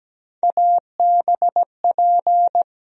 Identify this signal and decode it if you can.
Morse code, heard as a keyed tone, about 17 wpm, 700 Hz. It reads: ABP
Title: USA Call Sign Postfixes - 3 Letters 17wpm